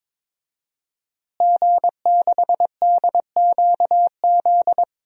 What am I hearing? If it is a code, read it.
Morse code, heard as a keyed tone, about 22 wpm, 700 Hz. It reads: G6DQZ
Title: DXCC-100 Call Signs - 5 Characters - Repeated 2x - 22wpm